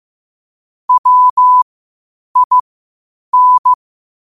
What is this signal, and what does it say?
Morse code, heard as a keyed tone, about 15 wpm, spaced as for 8 wpm, 1000 Hz. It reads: WIN
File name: qotdfiles / 8wpm